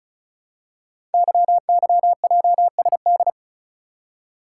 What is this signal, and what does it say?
Morse code, heard as a keyed tone, about 36 wpm, 700 Hz. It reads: YYJSD
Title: Code Groups - 5 Letters - 35wpm